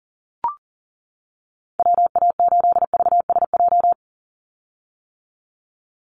Morse code, heard as a keyed tone, about 40 wpm, 700 Hz. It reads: WA8VSJ
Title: Call Signs - 40wpm